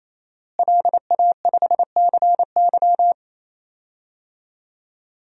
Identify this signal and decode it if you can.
Morse code, heard as a keyed tone, about 28 wpm, 700 Hz. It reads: LA5CY
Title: DXCC-100 Call Signs - 5 Characters - Repeated 3x - 28wpm